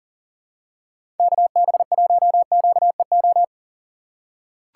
Morse code, heard as a keyed tone, about 38 wpm, 700 Hz. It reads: KB1QEO
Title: USA Call Signs - 6 Characters - 4x Character Spacing - 40wpm